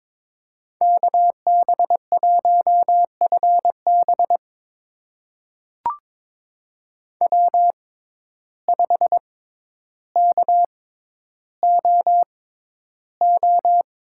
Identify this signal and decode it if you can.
Morse code, heard as a keyed tone, about 22 wpm, spaced as for 11 wpm, 700 Hz. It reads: KB1FB W5KOO
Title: USA Call Signs - 5 Characters - 6x Character Spacing - 22wpm